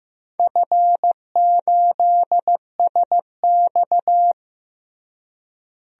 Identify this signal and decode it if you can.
Morse code, heard as a keyed tone, about 15 wpm, 700 Hz. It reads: F8SX
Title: DXCC-100 Call Signs - 4 Characters - Repeated 2x - 15wpm